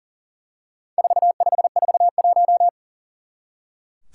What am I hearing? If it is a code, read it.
Morse code, heard as a keyed tone, about 38 wpm, 700 Hz. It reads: 4541
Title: Four Challenging Letter-Number Combinations 40wpm